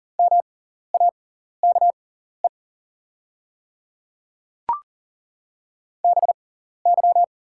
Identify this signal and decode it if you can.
Morse code, heard as a keyed tone, about 41 wpm, spaced as for 14 wpm, 690 Hz. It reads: MAKE BY
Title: 6x Character Spacing - Top 200 Words - 40wpm